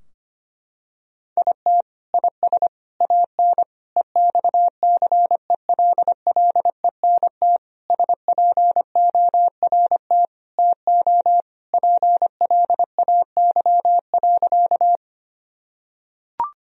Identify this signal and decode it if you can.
Morse code, heard as a keyed tone, about 25 wpm, 700 Hz. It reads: IT IS AN EXCELLENT SPORT TO PLAY.
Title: Sentences from Top 1000 Words - Review Part 4 - 25wpm